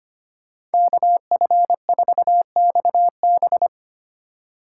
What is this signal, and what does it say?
Morse code, heard as a keyed tone, about 33 wpm, 700 Hz. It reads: KF4XB